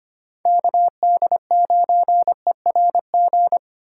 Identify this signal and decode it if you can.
Morse code, heard as a keyed tone, about 25 wpm, 700 Hz. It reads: KD9ERG
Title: USA Call Signs - 6 Characters - 4x Character Spacing - 25wpm